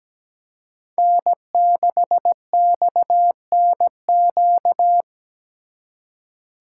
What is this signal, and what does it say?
Morse code, heard as a keyed tone, about 21 wpm, 700 Hz. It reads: N6XNQ
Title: Speed Racing - USA Call Signs 17wpm